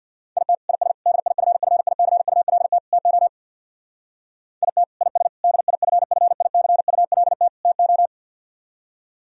Morse code, heard as a keyed tone, about 61 wpm, 700 Hz. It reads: IT IS DIFFICULT TO IT IS DIFFICULT TO
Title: Speed-Racing - Sets of 4 Words 50wpm